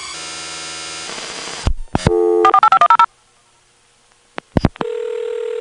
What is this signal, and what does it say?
DTMF, heard as telephone touch tones, tones of about 80 ms. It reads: 48510